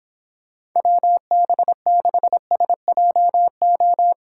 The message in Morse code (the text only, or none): WB6SJO